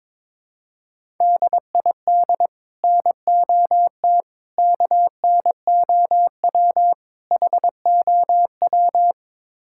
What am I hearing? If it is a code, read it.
Morse code, heard as a keyed tone, about 22 wpm, 700 Hz. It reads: DID NOT KNOW HOW
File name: Build A Phrase - 5-Word Phrases - 22wpm